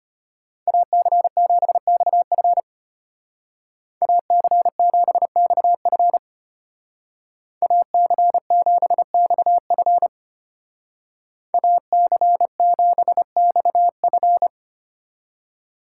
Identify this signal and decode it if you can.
Morse code, heard as a keyed tone, about 32 wpm, 700 Hz. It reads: AC7XF AC7XF AC7XF AC7XF